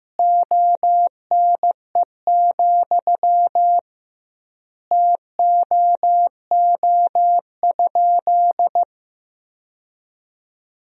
Morse code, heard as a keyed tone, about 15 wpm, 700 Hz. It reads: ONE, TOO?
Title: 2x Word-Spacing - Sentences from Top 200 Words - 15wpm